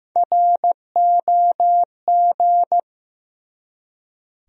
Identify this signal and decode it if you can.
Morse code, heard as a keyed tone, about 15 wpm, 700 Hz. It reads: ROG
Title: USA Common Names - Max 3 Letters - 15wpm